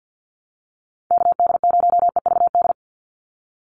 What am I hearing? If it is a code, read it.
Morse code, heard as a keyed tone, about 50 wpm, 700 Hz. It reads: KD0EVD
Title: USA Call Signs 50wpm